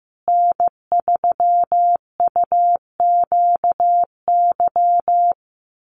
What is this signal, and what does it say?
Morse code, heard as a keyed tone, about 15 wpm, 700 Hz. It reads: N3UQY
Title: USA Call Signs Repeated 3x 15wpm